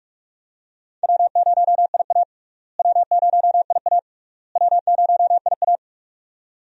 Morse code, heard as a keyed tone, about 45 wpm, 700 Hz. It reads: W0IA W0IA W0IA
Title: USA Call Signs - 4 Characters - Repeated 3x - 45wpm